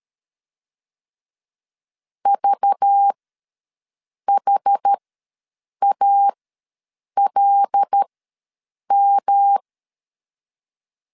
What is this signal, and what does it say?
Morse code, heard as a keyed tone, about 12 wpm, spaced as for 7 wpm, 800 Hz. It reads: VHALM
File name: LEZIONI / 10 SIGNS / 10C DOT ws 12wpm 3548